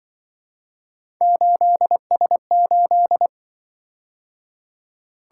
Morse code, heard as a keyed tone, about 25 wpm, 700 Hz. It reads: 8S8